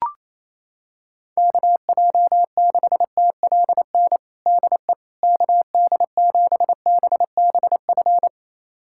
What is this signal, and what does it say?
Morse code, heard as a keyed tone, about 28 wpm, 700 Hz. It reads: KJ6TLN DE KD7BBF
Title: USA Callsign DE Callsign - 6 Characters - 28wpm